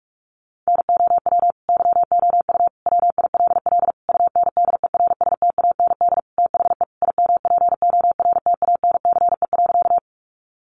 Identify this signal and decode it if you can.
Morse code, heard as a keyed tone, about 44 wpm, 700 Hz. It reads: NOW YOU WILL UNDERSTAND THE IMPORTANCE.
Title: Sentences from Top 1200 Words 45wpm